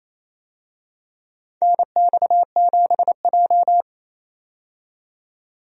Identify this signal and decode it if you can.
Morse code, heard as a keyed tone, about 28 wpm, 700 Hz. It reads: NX7J
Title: USA Call Signs - 4 Characters - 28wpm